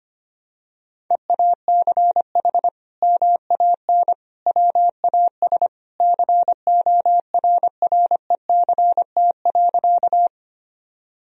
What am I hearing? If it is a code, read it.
Morse code, heard as a keyed tone, about 25 wpm, 700 Hz. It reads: EACH MAN WAS CORRECT.
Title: Sentences from Top 1000 Words - Review Part 2 - 25wpm